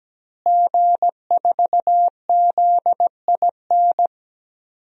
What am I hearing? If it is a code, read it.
Morse code, heard as a keyed tone, about 17 wpm, 700 Hz. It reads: G4ZIN
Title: DXCC-100 Call Signs - 5 Characters - Repeated 2x - 17wpm